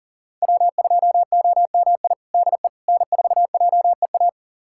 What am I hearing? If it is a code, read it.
Morse code, heard as a keyed tone, about 40 wpm, 700 Hz. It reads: W2OMI DE N4JEA
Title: USA Callsign DE Callsign - 5 Characters - 40wpm